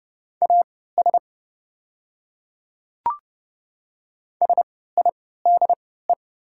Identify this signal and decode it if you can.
Morse code, heard as a keyed tone, about 30 wpm, spaced as for 17 wpm, 700 Hz. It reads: AS SIDE
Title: 3x Character Spacing - Top 100 Words - 30wpm